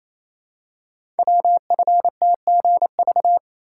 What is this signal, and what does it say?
Morse code, heard as a keyed tone, about 29 wpm, 700 Hz. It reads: WFTGV
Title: Code Groups - 5 Letters - 28wpm